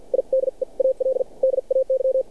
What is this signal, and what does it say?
Morse code, heard as a keyed tone, about 49 wpm, spaced as for 40 wpm, 530 Hz. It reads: IDEAL DAY
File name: archived / CRSnet